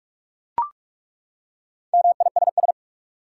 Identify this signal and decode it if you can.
Morse code, heard as a keyed tone, about 41 wpm, 700 Hz. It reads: MISS
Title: Four Letter Words - 45wpm